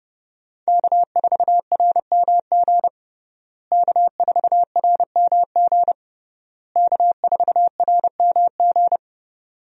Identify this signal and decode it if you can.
Morse code, heard as a keyed tone, about 30 wpm, 700 Hz. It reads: K4RMG K4RMG K4RMG
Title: USA Call Signs - 5 Characters - Repeated 3x - 30wpm